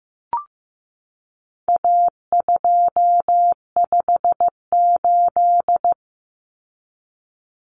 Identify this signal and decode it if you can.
Morse code, heard as a keyed tone, about 15 wpm, 700 Hz. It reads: A258
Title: Common Cut Numbers - 4-Digit 15wpm